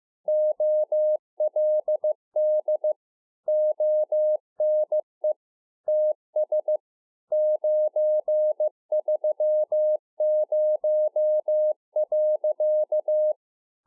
Morse code, heard as a keyed tone, about 15 wpm, 610 Hz. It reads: OLD ONE TS 930.
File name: 15WPM